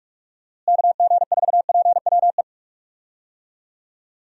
Morse code, heard as a keyed tone, about 46 wpm, 700 Hz. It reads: KG4PWE
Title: USA Call Signs - 6 Characters - Repeated 3x - 45wpm